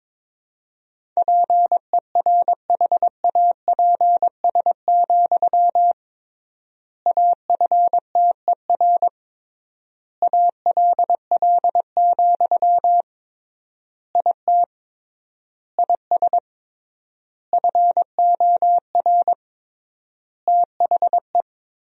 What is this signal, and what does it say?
Morse code, heard as a keyed tone, about 22 wpm, 700 Hz. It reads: PERHAPS, AFTER ALL, IT IS FOR THE
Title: 3x Word-Spacing - Sentences from Top 500 Words - 22wpm